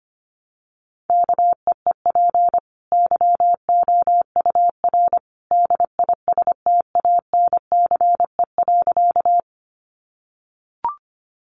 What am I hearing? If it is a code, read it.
Morse code, heard as a keyed tone, about 25 wpm, 700 Hz. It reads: KEEP YOUR DISTANCE.
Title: Sentences from Top 1200 Words 25wpm